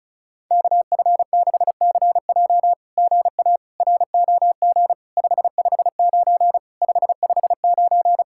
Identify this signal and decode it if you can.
Morse code, heard as a keyed tone, about 33 wpm, 700 Hz. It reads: KF6CJ GA ROG 559 559
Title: SOTA QSO Practice 35wpm